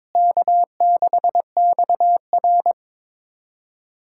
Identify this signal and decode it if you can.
Morse code, heard as a keyed tone, about 22 wpm, 700 Hz. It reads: K6XR
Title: K1USN SST QSO Practice - 22wpm